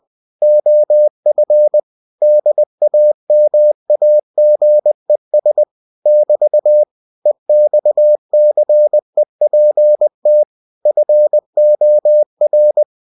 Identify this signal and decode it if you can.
Morse code, heard as a keyed tone, about 20 wpm, 600 Hz. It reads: OF DAMAGES <BT> EXCEPT FOR